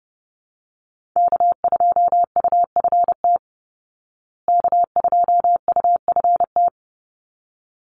Morse code, heard as a keyed tone, about 30 wpm, 700 Hz. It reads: K2UFT K2UFT
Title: Call Signs Repeated 3x - 30wpm